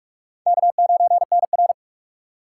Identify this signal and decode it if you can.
Morse code, heard as a keyed tone, about 46 wpm, 700 Hz. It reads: K9NR
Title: K1USN SST QSO Practice - 45wpm